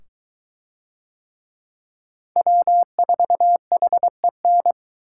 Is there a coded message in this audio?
Yes: Morse code from 2.4 to 4.7 s, heard as a keyed tone, about 22 wpm, 700 Hz. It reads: W4HEN